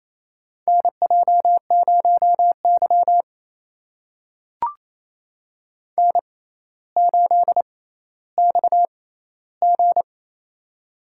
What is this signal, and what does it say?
Morse code, heard as a keyed tone, about 28 wpm, spaced as for 14 wpm, 700 Hz. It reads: NJ0Y N8XG
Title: USA Call Signs - 4 Characters - 6x Character Spacing - 28wpm